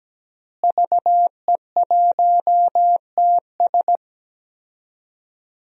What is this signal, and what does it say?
Morse code, heard as a keyed tone, about 17 wpm, 700 Hz. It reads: VE1TS
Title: DXCC-100 Call Signs - 5 Characters - Repeated 3x - 17wpm